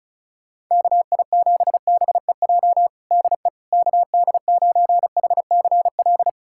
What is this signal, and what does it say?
Morse code, heard as a keyed tone, about 36 wpm, 700 Hz. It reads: KI7BEJ DE KD9HCL
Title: USA Callsign DE Callsign - 6 Characters - 35wpm